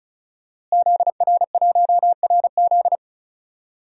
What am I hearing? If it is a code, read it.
Morse code, heard as a keyed tone, about 36 wpm, 700 Hz. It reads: ZR1RZ
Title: DXCC-100 Call Signs - 5 Characters - Repeated 3x - 35wpm